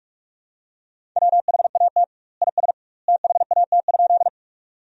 Morse code, heard as a keyed tone, about 43 wpm, 700 Hz. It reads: WHAT IS THAT?
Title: 2x Word-Spacing - Sentences from Top 100 Words - 45wpm